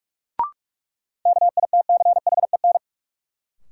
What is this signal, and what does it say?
Morse code, heard as a keyed tone, about 44 wpm, 700 Hz. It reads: KITCHEN
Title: Sentences from Top 1400 Words - New Word Drill - 45wpm